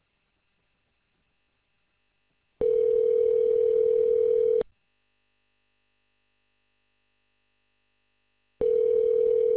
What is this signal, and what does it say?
Ringback tone (North America), 440 + 480 Hz.